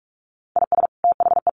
Morse code, heard as a keyed tone, about 44 wpm, 700 Hz. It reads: IS THE
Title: Sets of 5 Words Encore2 45wpm